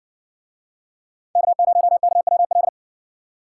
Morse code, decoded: K0KWL

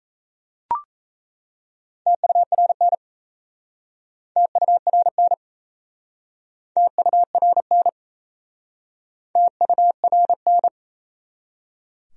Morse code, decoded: TURN TURN TURN TURN

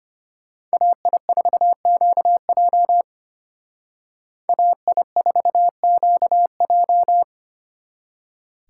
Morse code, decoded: AI4QJ AI4QJ

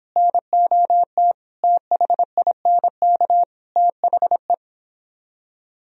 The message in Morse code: NOT THINK THE

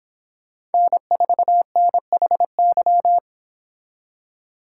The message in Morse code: N4NHY

